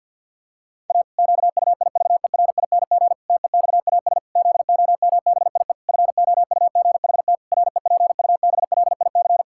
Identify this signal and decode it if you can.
Morse code, heard as a keyed tone, about 51 wpm, 700 Hz. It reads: A QUIVERING TEXAS ZOMBIE FOUGHT REPUBLIC
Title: Pangram Challenge v2 - 50wpm